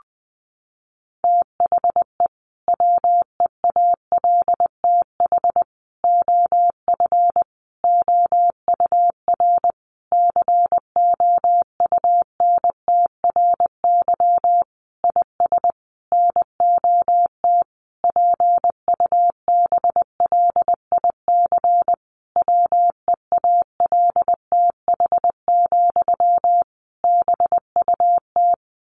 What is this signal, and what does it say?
Morse code, heard as a keyed tone, about 20 wpm, 700 Hz. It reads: THE WEALTH OF OUR COUNTRY IS NOT PUBLIC WEALTH, BUT
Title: Sentences from Top 1200 Words 20wpm